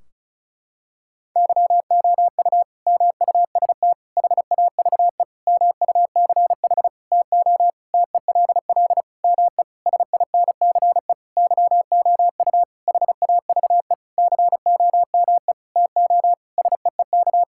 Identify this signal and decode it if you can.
Morse code, heard as a keyed tone, about 36 wpm, 700 Hz. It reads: YOU MUST HAVE MUCH TO TELL ME SINCE YOU HAVE COME TO SEEK
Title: Sentences from Top 1600 Words 35wpm